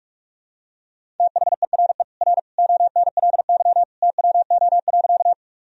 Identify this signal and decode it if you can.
Morse code, heard as a keyed tone, about 46 wpm, 700 Hz. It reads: THERE R ONLY TWO.
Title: Sentences from Top 200 Words and Abbreviations 45wpm